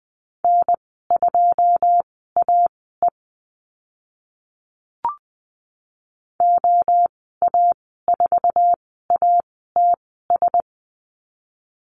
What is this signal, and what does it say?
Morse code, heard as a keyed tone, about 20 wpm, spaced as for 15 wpm, 700 Hz. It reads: N2AE OA4ATS
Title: 088 Lesson 15g - Number 5 - Call Signs 20wpm